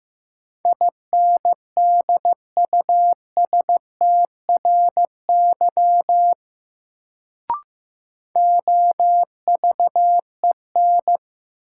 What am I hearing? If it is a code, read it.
Morse code, heard as a keyed tone, about 15 wpm, 700 Hz. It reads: INDUSTRY OVEN
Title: Sentences from Top 1000 Words - Review Part 4 - New Word Drill - 15wpm